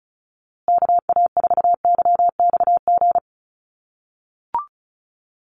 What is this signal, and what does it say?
Morse code, heard as a keyed tone, about 34 wpm, 700 Hz. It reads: KA4YXG
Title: USA Call Signs Repeated 2x 35wpm